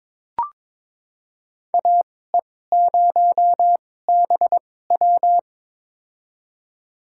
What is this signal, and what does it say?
Morse code, heard as a keyed tone, about 22 wpm, spaced as for 16 wpm, 700 Hz. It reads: AE0BW
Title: USA Call Signs - 2x Character Spacing - 22wpm